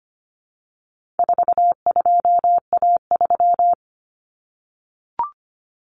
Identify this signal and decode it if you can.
Morse code, heard as a keyed tone, about 25 wpm, 700 Hz. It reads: 42A3
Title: Common Cut Numbers - 4-Digit 25wpm